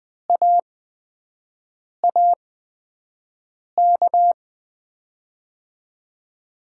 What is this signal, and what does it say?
Morse code, heard as a keyed tone, about 20 wpm, spaced as for 5 wpm, 700 Hz. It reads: AAK
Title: USA Call Signs - 5 Characters - 8x Character Spacing - 20wpm